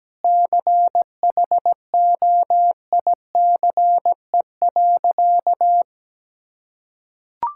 Morse code, heard as a keyed tone, about 17 wpm, 700 Hz. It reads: CHOICE.
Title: Sentences from Top 1000 Words - Review Part 6 - 17wpm